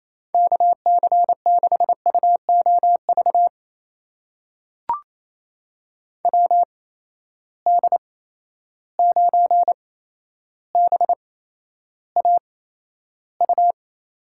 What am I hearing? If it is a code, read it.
Morse code, heard as a keyed tone, about 28 wpm, spaced as for 12 wpm, 700 Hz. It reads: KC6UOV WD9BAU
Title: USA Call Signs - 6 Characters - 8x Character Spacing - 28wpm